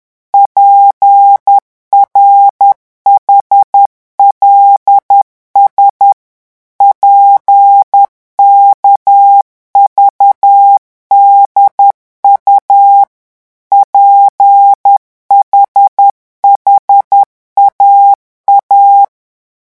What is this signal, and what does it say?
Morse code, heard as a keyed tone, about 10 wpm, 800 Hz. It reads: PRHLS PKVDU PHHAA
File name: morse mp3 / revision5-vitesse 10 mots